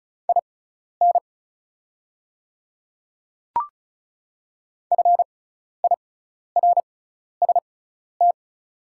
Morse code, heard as a keyed tone, about 34 wpm, spaced as for 12 wpm, 700 Hz. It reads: IN FIRST
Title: 6x Character Spacing - Top 300 Words - 35wpm